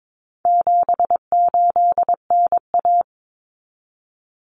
Morse code, heard as a keyed tone, about 22 wpm, 700 Hz. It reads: 78NA
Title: Common Cut Numbers - 4-Digit 22wpm